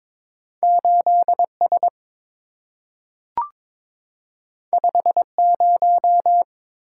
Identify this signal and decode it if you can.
Morse code, heard as a keyed tone, about 22 wpm, 700 Hz. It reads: 8S 50
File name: practice / 22wpm / Two Letter-Number Combinations - 22wpm